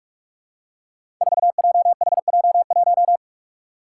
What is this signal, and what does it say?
Morse code, heard as a keyed tone, about 41 wpm, 700 Hz. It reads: 4JHJ1
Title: Five Challenging Letter-Number Combinations 45wpm